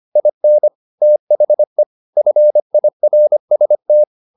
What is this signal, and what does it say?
Morse code, heard as a keyed tone, about 25 wpm, 600 Hz. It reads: IN THE FIRST